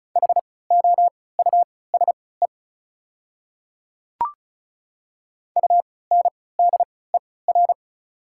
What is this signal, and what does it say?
Morse code, heard as a keyed tone, about 36 wpm, spaced as for 20 wpm, 700 Hz. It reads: HOUSE UNDER
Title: 3x Character Spacing - Top 200 Words - 35wpm